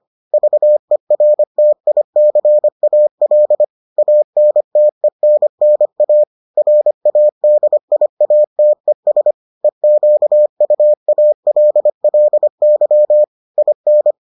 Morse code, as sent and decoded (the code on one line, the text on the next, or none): ...- . .-. - .. -.-. .- .-.. / .- -. - . -. -. .- / .-. .- -.. .. .- - . ... / . --.- ..- .- .-.. .-.. -.-- / .. -.
VERTICAL ANTENNA RADIATES EQUALLY IN